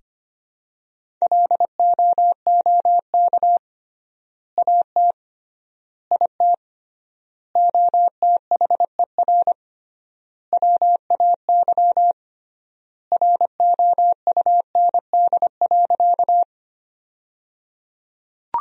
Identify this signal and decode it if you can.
Morse code, heard as a keyed tone, about 25 wpm, 700 Hz. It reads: LOOK AT IT OTHER WAY ROUND.
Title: 3x Word-Spacing - Sentences from Top 200 Words - 25wpm